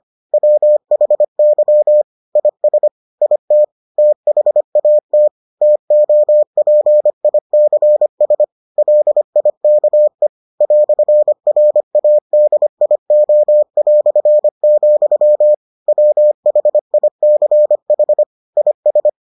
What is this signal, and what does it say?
Morse code, heard as a keyed tone, about 25 wpm, 600 Hz. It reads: WHY IS IT THAT TOPICS LIKE "RADIO", WHICH IS